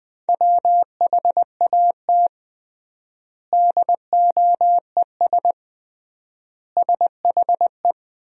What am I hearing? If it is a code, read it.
Morse code, heard as a keyed tone, about 20 wpm, 700 Hz. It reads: WHAT DOES SHE
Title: 3x Word-Spacing - Sentences from Top 200 Words - 20wpm